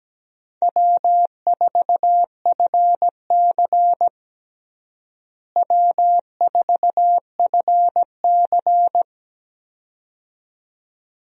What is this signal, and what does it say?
Morse code, heard as a keyed tone, about 17 wpm, 700 Hz. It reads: W4FC W4FC
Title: USA Call Signs - 4 Characters - Repeated 2x - 17wpm